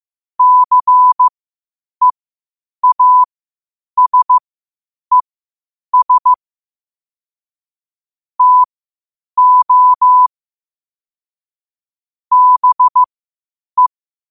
Morse code, heard as a keyed tone, about 15 wpm, spaced as for 8 wpm, 1000 Hz. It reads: CEASES TO BE
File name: qotdfiles / 8wpm